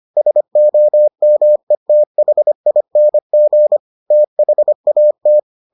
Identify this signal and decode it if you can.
Morse code, heard as a keyed tone, about 25 wpm, 600 Hz. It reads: SOMETHING THAT